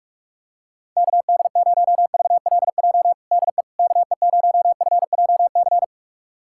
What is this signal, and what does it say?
Morse code, heard as a keyed tone, about 44 wpm, 700 Hz. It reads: KD0VLJ DE KE0FJC
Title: USA Callsign DE Callsign - 6 Characters - 45wpm